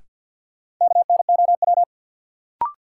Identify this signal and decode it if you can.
Morse code, heard as a keyed tone, about 53 wpm, 700 Hz. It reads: KNOW
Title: Sets of 2 Words - New Word Drill - 50wpm